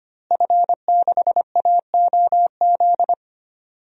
Morse code, heard as a keyed tone, about 25 wpm, 700 Hz. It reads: F6AOZ
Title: DXCC-100 Call Signs - 5 Characters - Repeated 3x - 25wpm